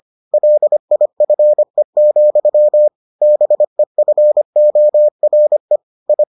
Morse code, heard as a keyed tone, about 25 wpm, 600 Hz. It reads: LIFE, BEFORE I